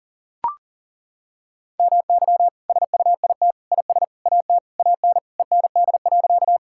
Morse code, heard as a keyed tone, about 40 wpm, 700 Hz. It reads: MY SUIT IS AT AN END.
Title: Sentences from Top 1600 Words 40wpm